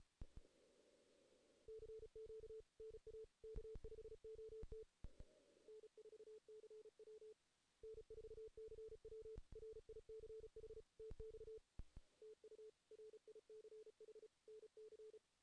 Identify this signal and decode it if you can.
Morse code, heard as a keyed tone, about 35 wpm, 440 Hz. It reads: CQ NA K5OT N4CW N4CW RICH TX TU RICH NC
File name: 2016 JAN CW NAQP / 2016 Jan naqp k5ot - 0-2